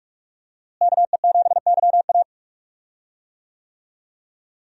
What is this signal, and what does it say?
Morse code, heard as a keyed tone, about 46 wpm, 700 Hz. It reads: KE7YA
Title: USA Call Signs - 5 Characters - Repeated 3x - 45wpm